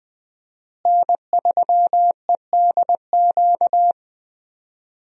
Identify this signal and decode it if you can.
Morse code, heard as a keyed tone, about 20 wpm, 700 Hz. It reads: N3EDQ
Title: USA Call Signs - 5 Characters - Repeated 2x - 20wpm